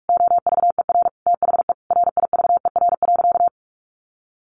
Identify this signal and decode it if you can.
Morse code, heard as a keyed tone, about 44 wpm, 700 Hz. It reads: OVER THE RIVER.
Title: Sentences from Top 1500 Words 45wpm